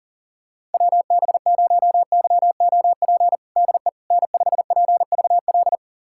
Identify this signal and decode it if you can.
Morse code, heard as a keyed tone, about 40 wpm, 700 Hz. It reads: WB0YOP DE N5PVL
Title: USA Callsign DE Callsign 40wpm